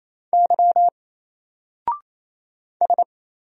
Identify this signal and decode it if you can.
Morse code, heard as a keyed tone, about 27 wpm, 700 Hz. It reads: Y S
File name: Single-Letter-Number-28wpm